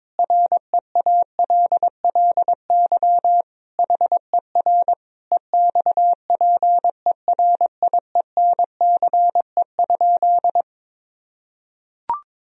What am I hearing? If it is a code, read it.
Morse code, heard as a keyed tone, about 22 wpm, 700 Hz. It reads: REALLY HER EXPERIENCE?
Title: Sentences from Top 1000 Words - Review Part 3 - 22wpm